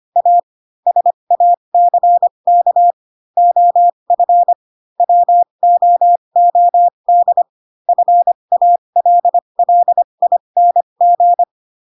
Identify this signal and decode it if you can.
Morse code, heard as a keyed tone, about 25 wpm, spaced as for 22 wpm, 700 Hz. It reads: A SACK OF WOOD FALLING
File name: Pinocchio / Pinocchio ch7 18 25 ik1hge